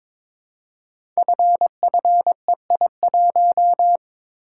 Morse code, heard as a keyed tone, about 22 wpm, 700 Hz. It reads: FFEI1
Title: Code Groups - 5 Letter:Numbers - 22wpm